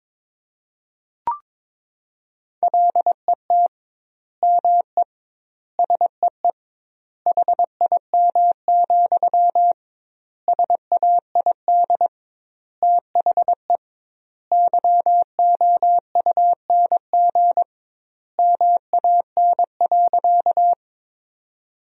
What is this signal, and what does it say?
Morse code, heard as a keyed tone, about 22 wpm, 700 Hz. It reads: LET ME SEE HIM, SAID THE YOUNG MAN.
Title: 2x Word-Spacing - Sentences from Top 400 Words - 22wpm